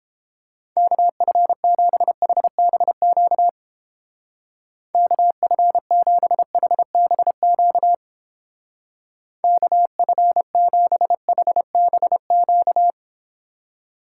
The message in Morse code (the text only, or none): KF7HBQ KF7HBQ KF7HBQ